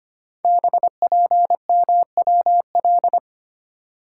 Morse code, heard as a keyed tone, about 25 wpm, 700 Hz. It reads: BPMWL
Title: Code Groups - 5 Letters - 25wpm